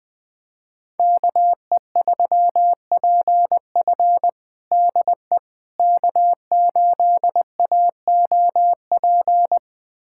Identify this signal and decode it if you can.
Morse code, heard as a keyed tone, about 20 wpm, 700 Hz. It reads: KE3PF DE K8AOP
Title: USA Callsign DE Callsign - 5 Characters - 20wpm